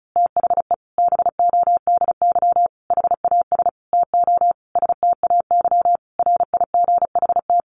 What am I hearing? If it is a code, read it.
Morse code, heard as a keyed tone, about 34 wpm, 700 Hz. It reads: THE BODY HAS TO STAY RIGHT
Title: A Sound Of Thunder 35wpm